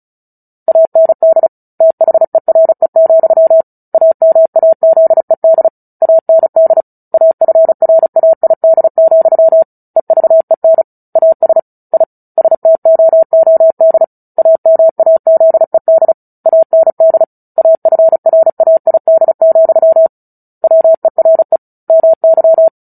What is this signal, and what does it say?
Morse code, heard as a keyed tone, about 34 wpm, 650 Hz. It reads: AND THERE, AMAZED AND AFRAID, EVEN AS I STOOD AMAZED AND AFRAID, WERE MY
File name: Book Two / 09-Chapter 9 - 35 WPM